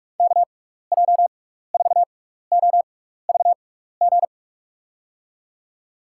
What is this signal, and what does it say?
Morse code, heard as a keyed tone, about 44 wpm, spaced as for 15 wpm, 700 Hz. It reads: KJ4OVG
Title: USA Call Signs - 6 Characters - 6x Character Spacing - 45wpm